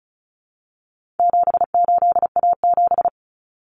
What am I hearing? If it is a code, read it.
Morse code, heard as a keyed tone, about 34 wpm, 700 Hz. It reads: 78A7
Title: Common Cut Numbers - 4-Digit 35wpm